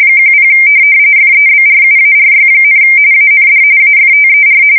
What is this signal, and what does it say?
RTTY, about 45.45 baud. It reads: 10 FEBRUARY, AND AFTER
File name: fsk170 / USS-Massachusetts